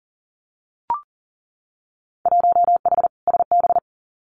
Morse code, heard as a keyed tone, about 41 wpm, 700 Hz. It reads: 1H SB
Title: Winter Field Day Exchanges 40wpm